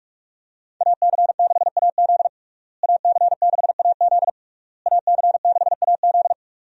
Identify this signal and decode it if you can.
Morse code, heard as a keyed tone, about 46 wpm, 700 Hz. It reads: AC6AZ AC6AZ AC6AZ
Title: USA Call Signs - 5 Characters - Repeated 3x - 45wpm